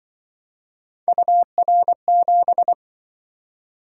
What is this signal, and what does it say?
Morse code, heard as a keyed tone, about 24 wpm, 700 Hz. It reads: UR7
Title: Speed-Racing - Three Letter-Number Combinations - 20wpm